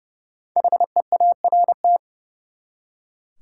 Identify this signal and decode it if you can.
Morse code, heard as a keyed tone, about 32 wpm, 700 Hz. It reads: HEART
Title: Five Letter Words - 30wpm